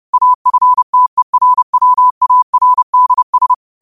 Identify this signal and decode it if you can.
Morse code, heard as a keyed tone, about 30 wpm, 1000 Hz. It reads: AFTERWARDS